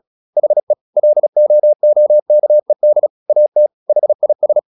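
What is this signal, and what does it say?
Morse code, heard as a keyed tone, about 36 wpm, 600 Hz. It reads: HE LOOKED AT HIS
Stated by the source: Morse practice at 36 wpm, text of Ken Follett's novel Never, from CW med Gnister.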